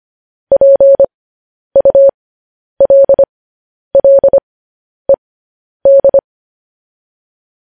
Morse code, heard as a keyed tone, about 25 wpm, spaced as for 10 wpm, 550 Hz. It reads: PULLED
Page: News Headlines in Morse Code at 10 WPM